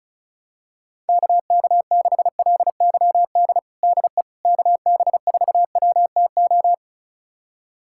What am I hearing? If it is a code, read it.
Morse code, heard as a keyed tone, about 34 wpm, 690 Hz. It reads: KK6LYD DE KB4WTO
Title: USA Callsign DE Callsign - 6 Characters - 35wpm